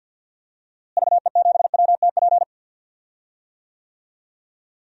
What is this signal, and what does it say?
Morse code, heard as a keyed tone, about 52 wpm, 700 Hz. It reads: VE7WTP
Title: DXCC-100 Call Signs - 6 Characters - Repeated 3x - 50wpm